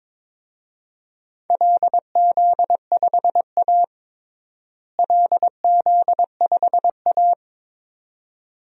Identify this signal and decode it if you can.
Morse code, heard as a keyed tone, about 22 wpm, 700 Hz. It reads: LZ5A LZ5A
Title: DXCC-100 Call Signs - 4 Characters - Repeated 2x - 22wpm